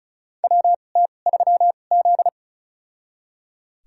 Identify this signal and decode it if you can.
Morse code, heard as a keyed tone, about 36 wpm, spaced as for 25 wpm, 700 Hz. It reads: WT3Z
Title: USA Call Signs - 4 Characters - 2x Character Spacing - 35wpm